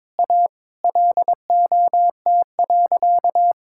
Morse code, heard as a keyed tone, about 22 wpm, 700 Hz. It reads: A LOT.
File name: practice / 22wpm / Sentences with 3-Word Phrases - 22wpm